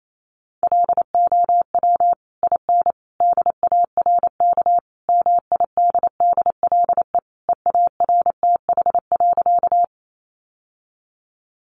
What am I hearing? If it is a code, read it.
Morse code, heard as a keyed tone, about 28 wpm, 700 Hz. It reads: LOW IN DARK MIDDLE EARTH.
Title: Sentences from Top 900 Words 28wpm